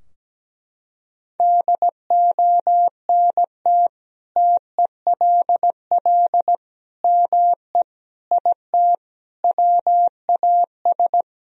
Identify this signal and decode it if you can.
Morse code, heard as a keyed tone, about 17 wpm, 700 Hz. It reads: DONT TELL ME IT WAS
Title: Sentences from Top 1000 Words - Review Part 3 - 17wpm